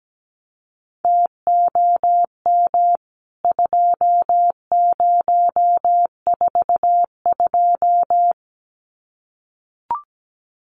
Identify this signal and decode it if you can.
Morse code, heard as a keyed tone, about 17 wpm, 700 Hz. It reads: TOM 2042
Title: CWT Exchange 17wpm